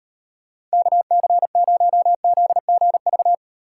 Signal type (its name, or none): Morse code